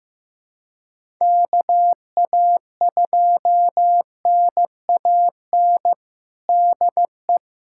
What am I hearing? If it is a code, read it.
Morse code, heard as a keyed tone, about 15 wpm, 700 Hz. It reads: KA2NAN DE
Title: USA Callsign DE Callsign 15wpm